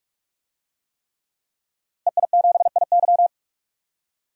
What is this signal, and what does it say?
Morse code, heard as a keyed tone, about 46 wpm, 700 Hz. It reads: EI7IY
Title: DXCC-100 Call Signs - 5 Characters - Repeated 3x - 45wpm